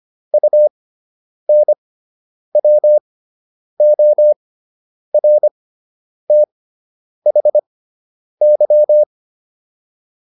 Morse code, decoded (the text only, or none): UNWORTHY